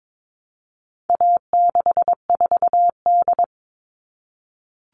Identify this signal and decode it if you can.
Morse code, heard as a keyed tone, about 22 wpm, 700 Hz. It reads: A64D
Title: All Cut Numbers - 4-Digit 22wpm